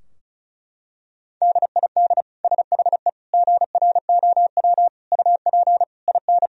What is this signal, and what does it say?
Morse code, heard as a keyed tone, about 34 wpm, 700 Hz. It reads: DID SHE GROW UP IN
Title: Sentences from Top 1000 Words - Review Part 2 - 35wpm